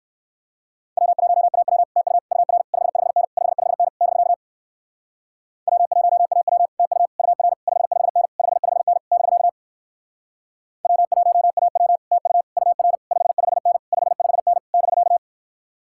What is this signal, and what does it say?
Morse code, heard as a keyed tone, about 57 wpm, 700 Hz. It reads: W1AW TU UR 55N 55N <BK> W1AW TU UR 55N 55N <BK> W1AW TU UR 55N 55N <BK>